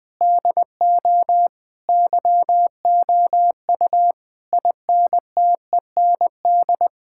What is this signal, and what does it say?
Morse code, heard as a keyed tone, about 21 wpm, 700 Hz. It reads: DO YOU INTEND
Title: Sentences from Top 1600 Words 20wpm